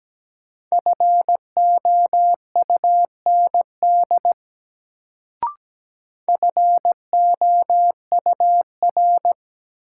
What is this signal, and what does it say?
Morse code, heard as a keyed tone, about 17 wpm, 700 Hz. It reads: FOUND FOUR
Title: Sentences from Top 300 Words - New Word Drill - 17wpm